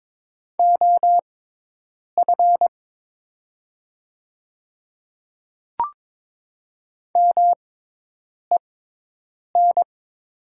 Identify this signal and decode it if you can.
Morse code, heard as a keyed tone, about 22 wpm, spaced as for 8 wpm, 700 Hz. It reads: OF MEN
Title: 6x Character Spacing - Top 200 Words - 22wpm